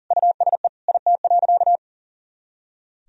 Morse code, heard as a keyed tone, about 37 wpm, 700 Hz. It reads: USE IT.
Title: Sentences from Top 500 Words and Abbreviations 40wpm